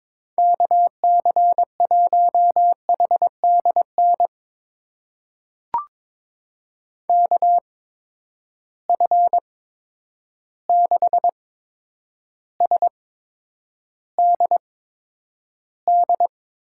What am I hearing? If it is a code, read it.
Morse code, heard as a keyed tone, about 22 wpm, spaced as for 9 wpm, 700 Hz. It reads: KC1HDN KF6SDD
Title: USA Call Signs - 6 Characters - 8x Character Spacing - 22wpm